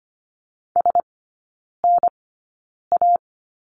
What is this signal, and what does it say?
Morse code, heard as a keyed tone, about 25 wpm, spaced as for 9 wpm, 700 Hz. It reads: SNA
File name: Lesson 15e - Number 5 - Three Characters 25wpm